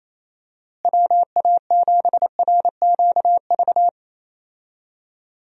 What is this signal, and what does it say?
Morse code, heard as a keyed tone, about 28 wpm, 700 Hz. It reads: WA7RQV
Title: USA Call Signs - 6 Characters - Repeated 3x - 28wpm